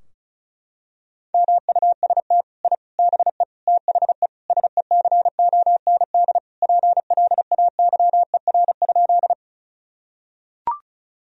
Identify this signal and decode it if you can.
Morse code, heard as a keyed tone, about 34 wpm, 700 Hz. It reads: MUST I BE THE SECOND PLAYER?
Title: Sentences from Top 1000 Words - Review Part 4 - 35wpm